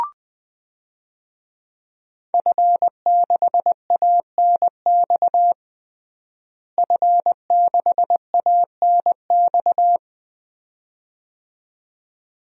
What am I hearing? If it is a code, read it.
Morse code, heard as a keyed tone, about 20 wpm, 700 Hz. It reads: F6ANX F6ANX
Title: DXCC-100 Call Signs - 5 Characters - Repeated 2x - 20wpm